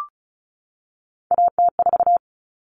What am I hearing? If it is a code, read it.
Morse code, heard as a keyed tone, about 34 wpm, 700 Hz. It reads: AT4
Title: Common Cut Numbers - 3-Digit 35wpm